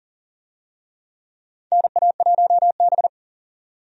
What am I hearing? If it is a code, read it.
Morse code, heard as a keyed tone, about 40 wpm, 700 Hz. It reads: NA1B
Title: USA Call Signs - 4 Characters - 40wpm